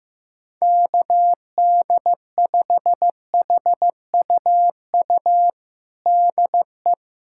Morse code, heard as a keyed tone, about 15 wpm, 700 Hz. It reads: KD5HUU DE
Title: USA Callsign DE Callsign 15wpm